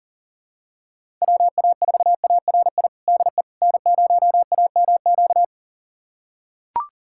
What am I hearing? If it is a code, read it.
Morse code, heard as a keyed tone, about 40 wpm, 700 Hz. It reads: WA4ARI DE N0AMQ
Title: USA Callsign DE Callsign 40wpm